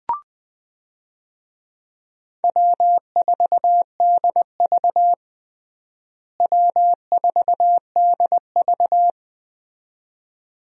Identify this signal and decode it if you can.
Morse code, heard as a keyed tone, about 21 wpm, 700 Hz. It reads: W4DV W4DV
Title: USA Call Signs - 4 Characters - Repeated 2x - 20wpm